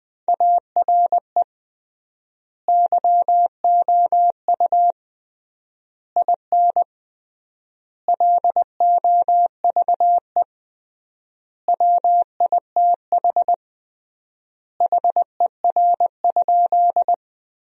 Morse code, heard as a keyed tone, about 20 wpm, 700 Hz. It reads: ARE YOU IN LOVE WITH HER?
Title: 3x Word-Spacing - Sentences from Top 400 Words - 20wpm